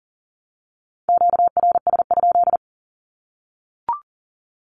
Morse code, heard as a keyed tone, about 40 wpm, 700 Hz. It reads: QRS?
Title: Q-Codes 40wpm